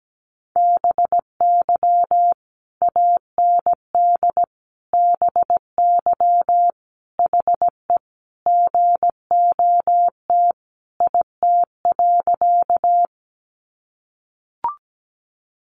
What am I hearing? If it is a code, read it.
Morse code, heard as a keyed tone, about 17 wpm, 700 Hz. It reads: BY AND BY HE GOT IT.
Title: Sentences from Top 300 Words 17wpm